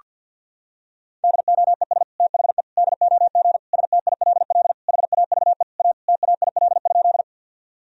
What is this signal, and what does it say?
Morse code, heard as a keyed tone, about 50 wpm, 700 Hz. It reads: DOES THE DOG STILL HAVE A TAIL?
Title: Sentences from Top 1000 Words - Review Part 3 - 50wpm